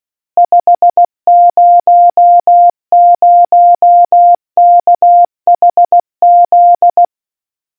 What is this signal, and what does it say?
Morse code, heard as a keyed tone, about 16 wpm, 700 Hz. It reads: 500KHZ